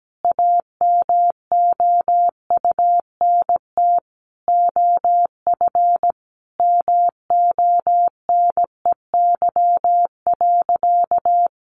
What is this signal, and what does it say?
Morse code, heard as a keyed tone, about 17 wpm, 700 Hz. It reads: AMOUNT OF MONEY.
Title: Sentences from Top 1300 Words 17wpm